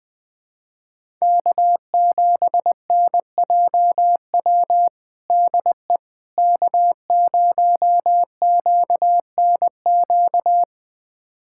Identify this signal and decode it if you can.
Morse code, heard as a keyed tone, about 20 wpm, 700 Hz. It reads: K7NJW DE K0QNQ
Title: USA Callsign DE Callsign - 5 Characters - 20wpm